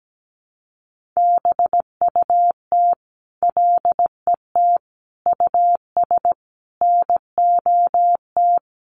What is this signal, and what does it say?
Morse code, heard as a keyed tone, about 17 wpm, 700 Hz. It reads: BUT LET US NOT
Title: Sentences from Top 1500 Words 17wpm